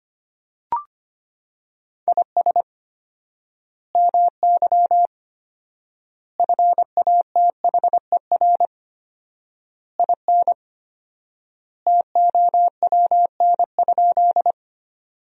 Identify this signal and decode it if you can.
Morse code, heard as a keyed tone, about 25 wpm, 700 Hz. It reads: IS MY FATHER IN TOWN?
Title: 4x Word-Spacing - Sentences from Top 500 Words - 25wpm